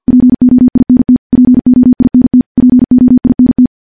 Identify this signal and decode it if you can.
Morse code, heard as a keyed tone, about 48 wpm, 250 Hz. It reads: POINT POINT POINT
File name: morse / morse-training-50WPM